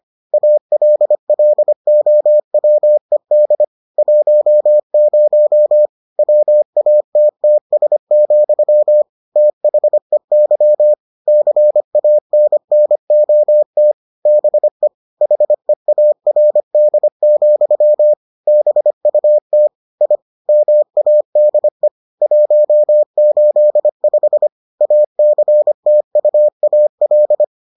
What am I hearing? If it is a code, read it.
Morse code, heard as a keyed tone, about 25 wpm, 600 Hz. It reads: ALLOWED 10 WATTS, THEY CANNOT BE HEARD, BUT I MADE 185 ACTUAL